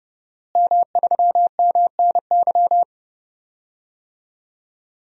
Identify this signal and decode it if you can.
Morse code, heard as a keyed tone, about 32 wpm, 700 Hz. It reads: M3MNY
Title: DXCC-100 Call Signs - 5 Characters - Repeated 3x - 30wpm